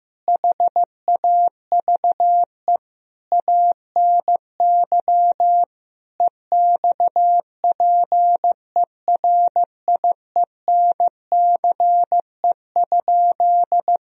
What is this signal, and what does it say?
Morse code, heard as a keyed tone, about 15 wpm, 700 Hz. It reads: HAVE ANY EXPERIENCE?
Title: Sentences from Top 1000 Words - Review Part 3 - 15wpm